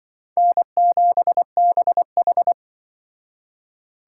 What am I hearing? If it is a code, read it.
Morse code, heard as a keyed tone, about 25 wpm, 700 Hz. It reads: N7BH